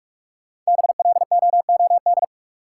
Morse code, heard as a keyed tone, about 44 wpm, 700 Hz. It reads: BLOOD
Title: Five Letter Words - 45wpm